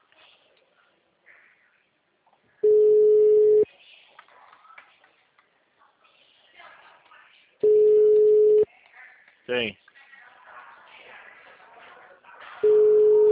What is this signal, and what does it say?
Ringback tone (Europe, 425 Hz), with speech.
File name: out- -2005- - - .4432